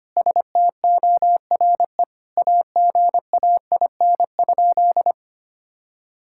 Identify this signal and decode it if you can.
Morse code, heard as a keyed tone, about 25 wpm, 700 Hz. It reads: STORE AGAIN?
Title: Sentences from Top 1000 Words - Review Part 5 - 25wpm